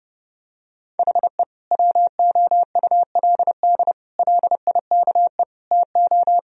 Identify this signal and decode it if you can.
Morse code, heard as a keyed tone, about 32 wpm, 700 Hz. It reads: HE WOULD LIKE TO
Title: Sentences from Top 1000 Words - Review Part 3 - 30wpm